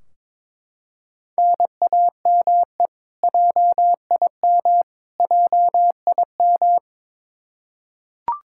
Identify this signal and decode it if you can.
Morse code, heard as a keyed tone, about 22 wpm, 700 Hz. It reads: NAME JIM JIM
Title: Practice QSO - Exchange 1 - Beginning - 22wpm